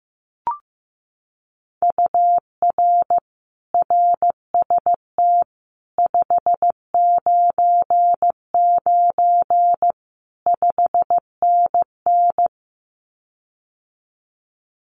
Morse code, heard as a keyed tone, about 15 wpm, 700 Hz. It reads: UR RST 599 5NN
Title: Common QSO Phrases Encore 15wpm